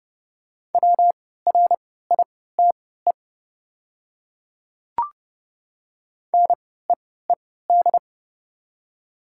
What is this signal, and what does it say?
Morse code, heard as a keyed tone, about 28 wpm, spaced as for 17 wpm, 690 Hz. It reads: WRITE NEED
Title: 3x Character Spacing - Top 200 Words - 30wpm